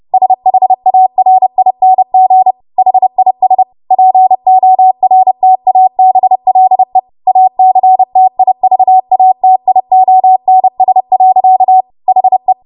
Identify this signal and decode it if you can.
Morse code, heard as a keyed tone, about 30 wpm, 750 Hz. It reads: SHARING HIS PORTABLE ACTIVATIONS. HE
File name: Morse / 30WPM